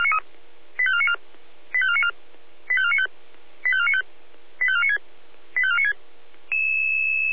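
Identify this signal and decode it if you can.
Five-tone selective call (ZVEI), heard as quick runs of five tones, tones of about 70 ms. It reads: 85494 85494 85495 85495 85496 85496
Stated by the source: ZVEI five-tone selective call received by SDR.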